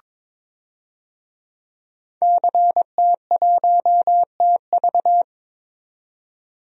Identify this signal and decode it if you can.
Morse code, heard as a keyed tone, about 22 wpm, 700 Hz. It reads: CT1TV
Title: DXCC-100 Call Signs - 5 Characters - 22wpm